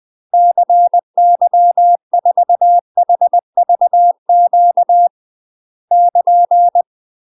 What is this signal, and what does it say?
Morse code, heard as a keyed tone, about 20 wpm, 700 Hz. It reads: CY4HVQ <KN>
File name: UK MASTER QSOs TRAINING / 20WPM / UK MASTER QSOs 0052